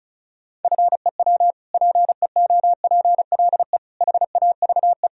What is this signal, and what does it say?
Morse code, heard as a keyed tone, about 36 wpm, 700 Hz. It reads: FEW PEOPLE HAVE
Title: Sentences from Top 1000 Words - Review Part 1 - 35wpm